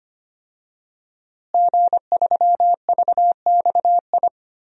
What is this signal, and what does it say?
Morse code, heard as a keyed tone, about 25 wpm, 700 Hz. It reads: G3VXI
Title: DXCC-100 Call Signs - 5 Characters - Repeated 2x - 25wpm